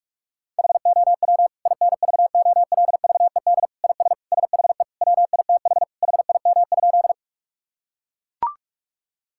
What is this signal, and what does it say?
Morse code, heard as a keyed tone, about 41 wpm, 700 Hz. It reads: HOW INVOLVED IS SHE WITH HIM?
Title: Sentences from Top 1000 Words - Review Part 5 - 45wpm